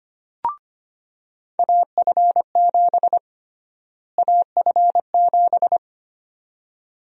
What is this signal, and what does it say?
Morse code, heard as a keyed tone, about 25 wpm, 700 Hz. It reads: AF7 AF7
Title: USA Call Sign Prefixes - Repeated 2x - 25wpm